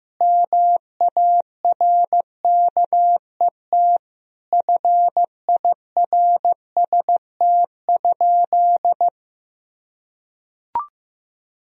Morse code, heard as a keyed tone, about 15 wpm, 700 Hz. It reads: MARKET FIRST?
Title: Sentences from Top 1000 Words - Review Part 3 - 15wpm